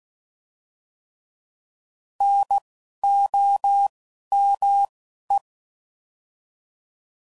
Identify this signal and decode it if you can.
Morse code, heard as a keyed tone, about 16 wpm, 800 Hz. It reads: N O M E